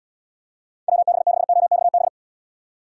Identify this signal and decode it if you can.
Morse code, heard as a keyed tone, about 72 wpm, 700 Hz. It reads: WB6OXB